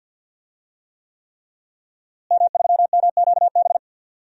Morse code, heard as a keyed tone, about 52 wpm, 700 Hz. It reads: M3MQB
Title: DXCC-100 Call Signs - 5 Characters - Repeated 3x - 50wpm